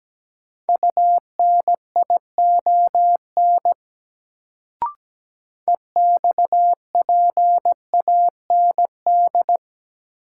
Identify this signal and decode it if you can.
Morse code, heard as a keyed tone, about 17 wpm, 700 Hz. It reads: UNION EXPAND
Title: Sentences from Top 1400 Words - New Word Drill - 17wpm